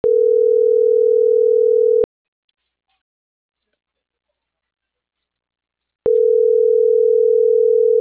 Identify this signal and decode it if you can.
Ringback tone (North America), 440 + 480 Hz.
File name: CENTRAL MAERSK / out yunuel.morales